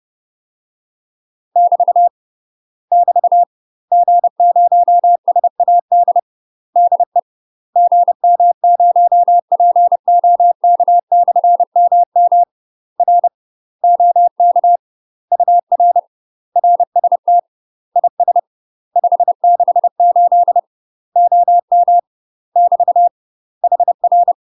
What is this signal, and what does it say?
Morse code, heard as a keyed tone, about 30 wpm, 700 Hz. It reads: <BT> <BT> G0SAD DE GM0POK/MM R OK UR RST IS 568 OM <BT> HR
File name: UK MASTER QSOs TRAINING / 30WPM / UK MASTER QSOs 0051